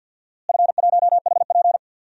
Morse code, heard as a keyed tone, about 52 wpm, 700 Hz. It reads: F1HP
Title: DXCC-100 Call Signs - 4 Characters - Repeated 3x - 50wpm